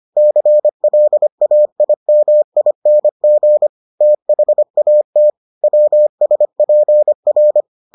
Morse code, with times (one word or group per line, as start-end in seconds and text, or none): 0.2-3.7 CLAIMING
4.0-5.3 THAT
5.6-7.6 WSPR